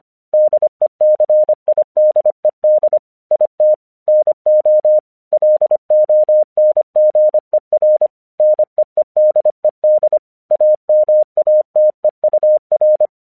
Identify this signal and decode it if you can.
Morse code, heard as a keyed tone, about 24 wpm, 610 Hz. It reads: DECIDED IT NO LONGER NEEDED AMATEUR
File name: 1 English 25wpm / AQC06